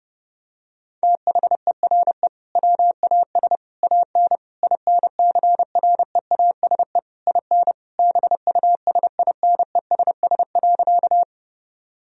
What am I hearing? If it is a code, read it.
Morse code, heard as a keyed tone, about 30 wpm, 700 Hz. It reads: THERE WAS AN INCREASE IN BUSINESS.
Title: Sentences from Top 1000 Words - Review Part 4 - 30wpm